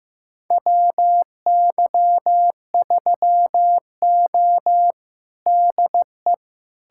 Morse code, heard as a keyed tone, about 15 wpm, 700 Hz. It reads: WY3O DE
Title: USA Callsign DE Callsign - 4 Characters - 15wpm